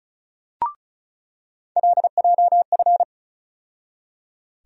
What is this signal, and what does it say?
Morse code, heard as a keyed tone, about 36 wpm, 700 Hz. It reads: LJF
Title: USA Call Sign Postfixes - 3 Letters 35wpm